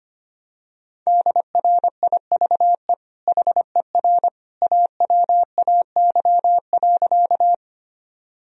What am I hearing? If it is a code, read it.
Morse code, heard as a keyed tone, about 25 wpm, 700 Hz. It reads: DRIVE HER AWAY.
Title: Sentences from Top 1000 Words - Review Part 2 - 25wpm